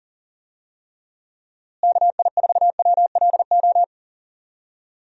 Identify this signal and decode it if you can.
Morse code, heard as a keyed tone, about 40 wpm, 700 Hz. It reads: KI4WLO
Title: USA Call Signs - 6 Characters - Repeated 2x - 40wpm